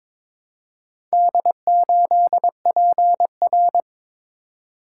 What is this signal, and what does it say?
Morse code, heard as a keyed tone, about 22 wpm, 700 Hz. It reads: D8PR
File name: practice / 22wpm / Four Letter-Number Combinations - 22wpm